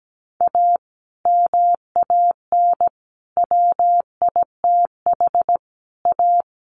Morse code, heard as a keyed tone, about 17 wpm, 700 Hz. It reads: A MAN WITH A
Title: Sentences from Top 1100 Words 17wpm